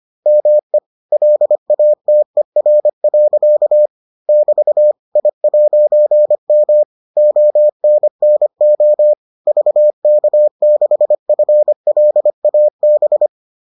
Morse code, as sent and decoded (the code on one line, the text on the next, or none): -- . / .-.. .- - . .-. .-.-.- / -...- / .. .----. -- / --- -. -. --- / ...- -.- -.... ..-. .-.. .- -...
ME LATER. <BT> I'M ONNO VK6FLAB